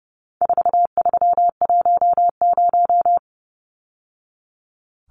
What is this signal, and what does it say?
Morse code, heard as a keyed tone, about 30 wpm, 700 Hz. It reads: 4310
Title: Numbers - 4-Digit 30wpm